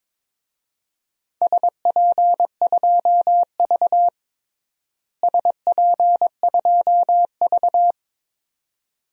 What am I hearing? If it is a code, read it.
Morse code, heard as a keyed tone, about 21 wpm, 700 Hz. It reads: SP2V SP2V
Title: DXCC-100 Call Signs - 4 Characters - Repeated 2x - 22wpm